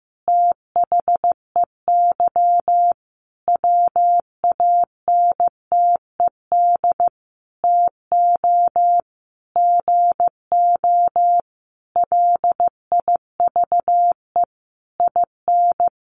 Morse code, heard as a keyed tone, about 15 wpm, 700 Hz. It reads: THEY WANTED TO GO LIVE IN
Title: A Sound Of Thunder 15wpm